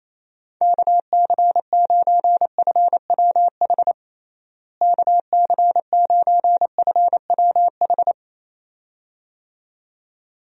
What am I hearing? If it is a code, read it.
Morse code, heard as a keyed tone, about 28 wpm, 700 Hz. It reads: KC9FWH KC9FWH
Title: USA Call Signs - 6 Characters - Repeated 2x - 28wpm